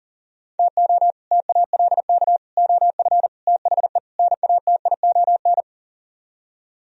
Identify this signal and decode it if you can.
Morse code, heard as a keyed tone, about 50 wpm, 700 Hz. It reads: TO TALK OF THE NATION
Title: Speed-Racing - Sets of 5 Words 40wpm